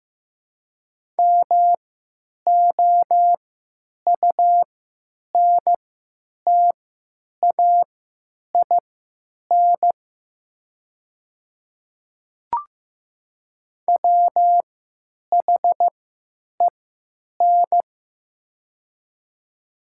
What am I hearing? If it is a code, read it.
Morse code, heard as a keyed tone, about 15 wpm, spaced as for 9 wpm, 700 Hz. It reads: MOUNTAIN WHEN
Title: 3x Character Spacing - Top 300 Words - 15wpm